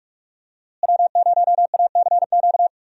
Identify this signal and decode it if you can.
Morse code, heard as a keyed tone, about 46 wpm, 700 Hz. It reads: W0ACQ
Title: USA Call Signs - 5 Characters - Repeated 3x - 45wpm